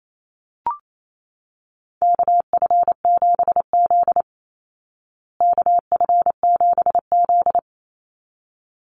Morse code, heard as a keyed tone, about 28 wpm, 700 Hz. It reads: KF7Z KF7Z
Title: Call Signs Repeated 3x - 28wpm